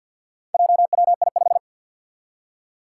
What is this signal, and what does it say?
Morse code, heard as a keyed tone, about 52 wpm, 700 Hz. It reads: JWI5
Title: Four Challenging Letter-Number Combinations 50wpm